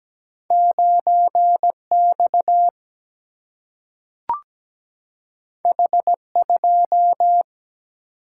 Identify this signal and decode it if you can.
Morse code, heard as a keyed tone, about 17 wpm, 700 Hz. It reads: 9X H2
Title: Single-Letter Single-Number 17wpm